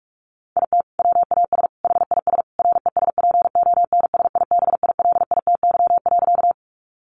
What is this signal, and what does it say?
Morse code, heard as a keyed tone, about 44 wpm, 700 Hz. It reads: IT WAS HIS RESPONSIBILITY.
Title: Sentences from Top 1200 Words 45wpm